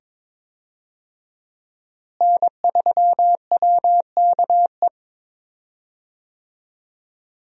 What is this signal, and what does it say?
Morse code, heard as a keyed tone, about 22 wpm, 700 Hz. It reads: N3WKE
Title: USA Call Signs - 5 Characters - Repeated 2x - 22wpm